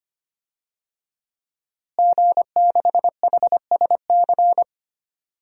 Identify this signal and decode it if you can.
Morse code, heard as a keyed tone, about 25 wpm, 700 Hz. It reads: G6HSC
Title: DXCC-100 Call Signs - 5 Characters - Repeated 2x - 25wpm